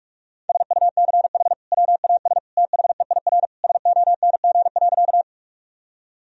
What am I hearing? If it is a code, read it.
Morse code, heard as a keyed tone, about 46 wpm, 700 Hz. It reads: SUCH WAS THEIR SONG.
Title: Sentences from Top 1000 Words - Review Part 2 - 45wpm